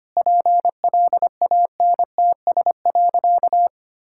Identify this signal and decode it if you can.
Morse code, heard as a keyed tone, about 25 wpm, 700 Hz. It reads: PLANTS.
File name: practice / 25wpm / Sentences with 4-Word Phrases - 25wpm